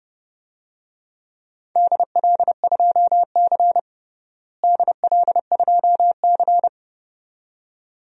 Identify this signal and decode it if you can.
Morse code, heard as a keyed tone, about 29 wpm, 700 Hz. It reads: DL2C DL2C
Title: DXCC-100 Call Signs - 4 Characters - Repeated 2x - 30wpm